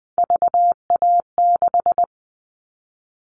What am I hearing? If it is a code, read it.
Morse code, heard as a keyed tone, about 20 wpm, 700 Hz. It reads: VA6
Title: Callsign Prefixes for Top 100 DXCC - 20wpm